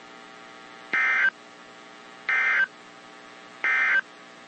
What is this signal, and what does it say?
Emergency Alert System (SAME) tones, heard as harsh buzzing bursts. It is heard from 0.8 to 4.1 s.
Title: WZ2523 - NOAA Weather Radio